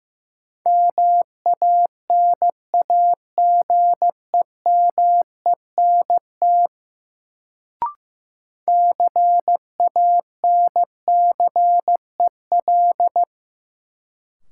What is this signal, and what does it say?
Morse code, heard as a keyed tone, about 15 wpm, 700 Hz. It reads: MANAGEMENT CANCEL
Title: Sentences from Top 1200 Words - New Word Drill Alt - 15wpm